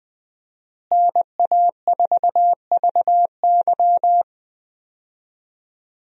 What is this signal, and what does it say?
Morse code, heard as a keyed tone, about 20 wpm, 700 Hz. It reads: NA4VY
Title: USA Call Signs - 5 Characters - Repeated 2x - 20wpm